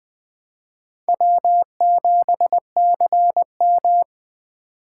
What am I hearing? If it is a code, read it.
Morse code, heard as a keyed tone, about 24 wpm, 700 Hz. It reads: W7CM
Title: Speed Racing - USA Call Signs 20wpm